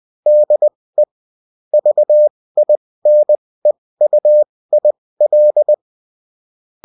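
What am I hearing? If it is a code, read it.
Morse code, heard as a keyed tone, about 20 wpm, spaced as for 16 wpm, 600 Hz. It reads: DE VINEUIL
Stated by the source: Morse code bulletin of a French amateur CW union, sent as text and in CW.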